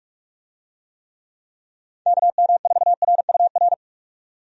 Morse code, heard as a keyed tone, about 46 wpm, 700 Hz. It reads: KM4RUR
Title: USA Call Signs - 6 Characters - 45wpm